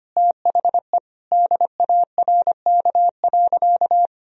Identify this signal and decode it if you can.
Morse code, heard as a keyed tone, about 25 wpm, 700 Hz. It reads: THE DARK.
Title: Sentences from Top 1000 Words - Review Part 4 - 25wpm